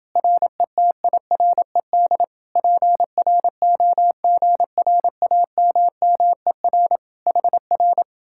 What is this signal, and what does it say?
Morse code, heard as a keyed tone, about 30 wpm, 700 Hz. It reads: RETIRED PROGRAMMER HR